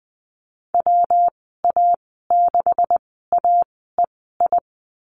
Morse code, heard as a keyed tone, about 20 wpm, spaced as for 15 wpm, 700 Hz. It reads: WA6AEI
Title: 173 Lesson 28g - Number 9 - Call Signs 20wpm